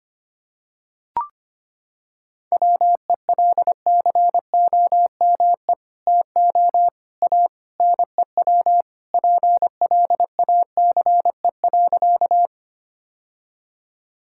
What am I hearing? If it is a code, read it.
Morse code, heard as a keyed tone, about 25 wpm, 700 Hz. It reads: WELCOME TO A NEW PLACE.
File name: practice / 25wpm / Sentences with 2-Word Phrases - 25wpm